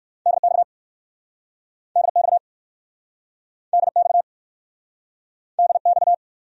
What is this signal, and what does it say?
Morse code, heard as a keyed tone, about 63 wpm, 700 Hz. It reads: DX DX DX DX